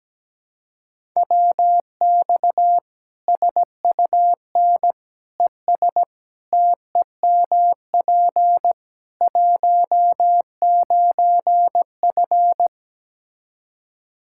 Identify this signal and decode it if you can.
Morse code, heard as a keyed tone, about 17 wpm, 700 Hz. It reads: WX SUN ES TEMP 19F
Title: Practice QSO - Exchange 2 - Info - 17wpm